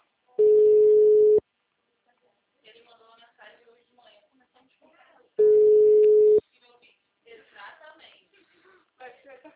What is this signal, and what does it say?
Ringback tone (Europe, 425 Hz), with speech.
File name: out- -2006- - - .266